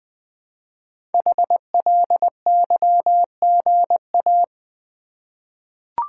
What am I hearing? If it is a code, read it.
Morse code, heard as a keyed tone, about 20 wpm, 700 Hz. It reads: HLYGA
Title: Code Groups - 5 Letters - 20wpm